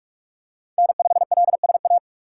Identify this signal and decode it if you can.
Morse code, heard as a keyed tone, about 46 wpm, 700 Hz. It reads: N5LSA
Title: USA Call Signs - 2x Character Spacing - 45wpm